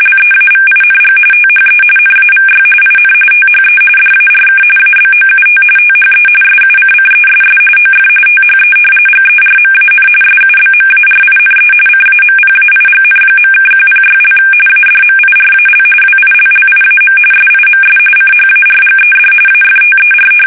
RTTY, about 45.45 baud. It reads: LEAVING THE NAVY IN THE DARK ABOUT THE SUCCESS OF THE GERMAN SUBMARINE CAMPAIGN.(106) ADMIRAL WILLIAM SIMS CHARGED AFTER
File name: fsk850